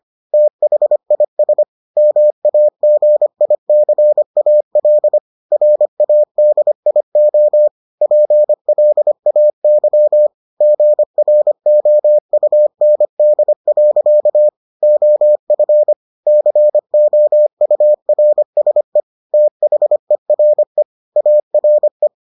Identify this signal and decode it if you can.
Morse code, heard as a keyed tone, about 25 wpm, 600 Hz. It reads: THIS MAGICAL RADIO PLAYGROUND. OF COURSE THERE ARE